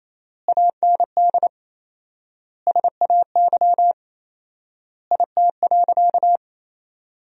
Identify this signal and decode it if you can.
Morse code, heard as a keyed tone, about 28 wpm, 700 Hz. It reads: AND SAY IT.
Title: 4x Word-Spacing - Sentences from Top 400 Words - 28wpm